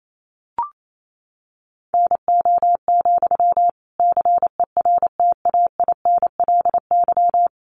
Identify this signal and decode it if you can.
Morse code, heard as a keyed tone, about 28 wpm, 700 Hz. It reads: NO, CERTAINLY
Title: Sentences from Top 900 Words 28wpm